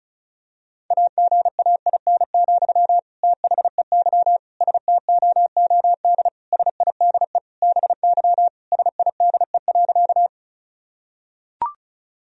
Morse code, heard as a keyed tone, about 34 wpm, 700 Hz. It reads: AGAIN, THEY STOOD SIDE BY SIDE.
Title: Sentences from Top 1000 Words - Review Part 2 - 35wpm